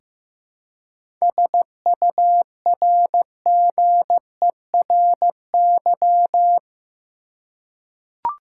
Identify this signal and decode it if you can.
Morse code, heard as a keyed tone, about 15 wpm, 700 Hz. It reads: SURGERY
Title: Seven Letter Words - 15wpm